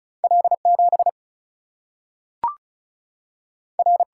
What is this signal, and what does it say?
Morse code, heard as a keyed tone, about 36 wpm, 700 Hz. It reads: L 7 R
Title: Single-Letter Single-Number 35wpm